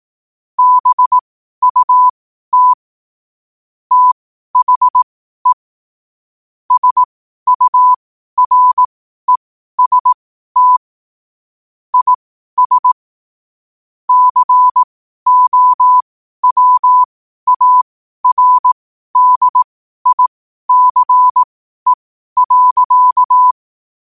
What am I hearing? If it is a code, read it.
Morse code, heard as a keyed tone, about 18 wpm, spaced as for 12 wpm, 1000 Hz. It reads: BUT THE SUREST IS COWARDICE.
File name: qotdfiles / 12wpm